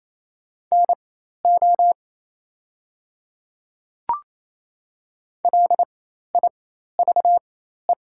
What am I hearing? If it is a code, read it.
Morse code, heard as a keyed tone, about 27 wpm, spaced as for 13 wpm, 690 Hz. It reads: NO LIVE
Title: 4x Character Spacing - Top 200 Words - 28wpm